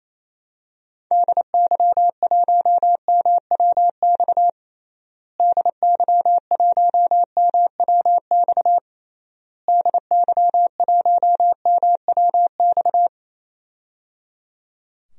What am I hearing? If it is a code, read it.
Morse code, heard as a keyed tone, about 28 wpm, 700 Hz. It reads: DY1MWX DY1MWX DY1MWX
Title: DXCC-100 Call Signs - 6 Characters - Repeated 3x - 28wpm